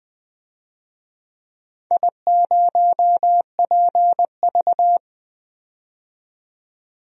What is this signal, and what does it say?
Morse code, heard as a keyed tone, about 20 wpm, 700 Hz. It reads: I0PV
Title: DXCC-100 Call Signs - 4 Characters - 20wpm